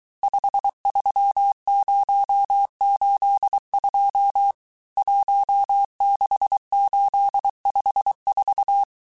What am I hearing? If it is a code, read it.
Morse code, heard as a keyed tone, about 23 wpm, 800 Hz. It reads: 53082 16854
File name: cours-du-mercredi / audios / cours-855-texte-chiffre-vitesse-22-mpm